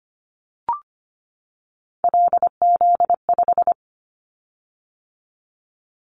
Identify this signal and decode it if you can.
Morse code, heard as a keyed tone, about 25 wpm, 700 Hz. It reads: LZ5
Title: Callsign Prefixes for Top 100 DXCC - 25wpm